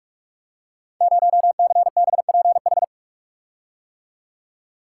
Morse code, heard as a keyed tone, about 44 wpm, 700 Hz. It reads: 0CBPH
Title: Code Groups - 5 Letter:Numbers - 45wpm